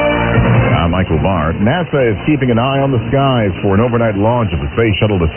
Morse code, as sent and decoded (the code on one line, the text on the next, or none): none